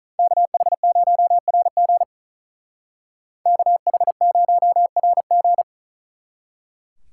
Morse code, heard as a keyed tone, about 45 wpm, 700 Hz. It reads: KH0RG KH0RG